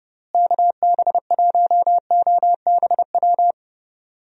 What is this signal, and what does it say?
Morse code, heard as a keyed tone, about 30 wpm, 700 Hz. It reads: KB1OBW
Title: USA Call Signs - 6 Characters - Repeated 3x - 30wpm